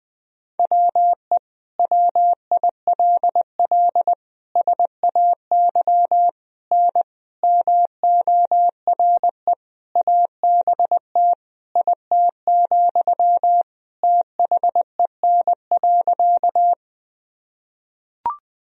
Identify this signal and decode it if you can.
Morse code, heard as a keyed tone, about 20 wpm, 700 Hz. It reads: WE WILL SAY N MORE ABT IT, THEN.
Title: Sentences from Top 200 Words and Abbreviations 20wpm